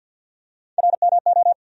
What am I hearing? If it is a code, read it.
Morse code, heard as a keyed tone, about 51 wpm, 700 Hz. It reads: RMO
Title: USA Call Sign Postfixes - 3 Letters 50wpm